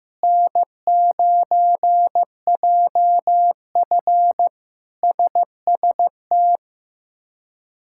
Morse code, heard as a keyed tone, about 15 wpm, 700 Hz. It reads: N9JF SST
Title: K1USN SST QSO Practice - 15wpm